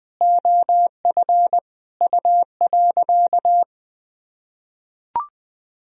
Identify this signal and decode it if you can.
Morse code, heard as a keyed tone, about 20 wpm, 700 Hz. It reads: OF U.
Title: Sentences from Top 200 Words and Abbreviations 20wpm